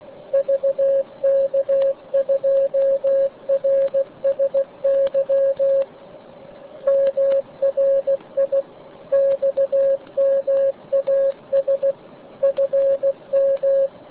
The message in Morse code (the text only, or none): VK2RSY MRI XMAS FM